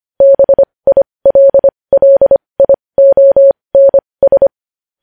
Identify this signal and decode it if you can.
Morse code, heard as a keyed tone, about 25 wpm, spaced as for 20 wpm, 550 Hz. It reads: BILLIONS
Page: news-headlines-in-morse-code-at-20-wpm